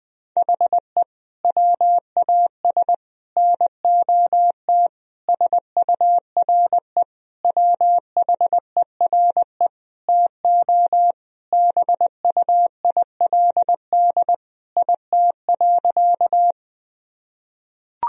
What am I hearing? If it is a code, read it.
Morse code, heard as a keyed tone, about 21 wpm, 700 Hz. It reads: HE WAS NOT SURE WHERE TO BUILD IT.
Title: Sentences from Top 1000 Words - Review Part 1 - 20wpm